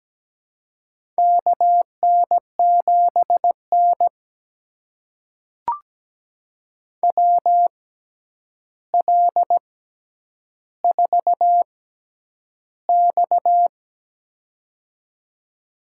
Morse code, heard as a keyed tone, about 17 wpm, spaced as for 9 wpm, 700 Hz. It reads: KN7N WL4X
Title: USA Call Signs - 4 Characters - 6x Character Spacing - 17wpm